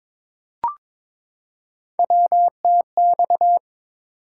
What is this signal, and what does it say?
Morse code, heard as a keyed tone, about 22 wpm, 700 Hz. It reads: WTX
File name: practice / 22wpm / ARRL Sections v2 - 22wpm